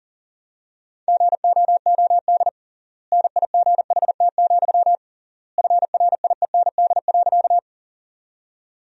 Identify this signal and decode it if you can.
Morse code, heard as a keyed tone, about 40 wpm, 700 Hz. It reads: GOOD NIGHT, FRIEND.
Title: 3x Word-Spacing - Sentences from Top 300 Words - 40wpm